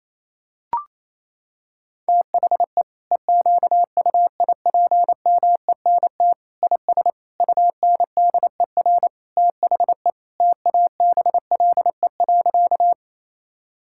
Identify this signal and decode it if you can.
Morse code, heard as a keyed tone, about 28 wpm, 700 Hz. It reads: THE EQUIPMENT IS UNDER THE TABLE.
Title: Sentences from Top 1000 Words - Review Part 5 - 28wpm